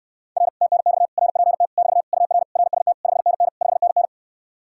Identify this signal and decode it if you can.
Morse code, heard as a keyed tone, about 69 wpm, 700 Hz. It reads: R TNX RPT <BT> UR RST 5NN 5NN